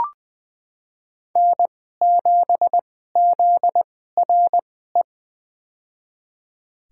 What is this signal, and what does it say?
Morse code, heard as a keyed tone, about 20 wpm, spaced as for 15 wpm, 700 Hz. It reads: N7ZRE
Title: USA Call Signs - 5 Characters - 2x Character Spacing - 20wpm